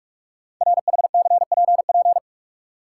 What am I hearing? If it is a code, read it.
Morse code, heard as a keyed tone, about 44 wpm, 690 Hz. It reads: RHCPP
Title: Code Groups - 5 Letters - 45wpm